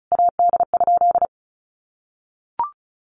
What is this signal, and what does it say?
Morse code, heard as a keyed tone, about 34 wpm, 700 Hz. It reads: AD?
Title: Sentences from Top 1300 Words 35wpm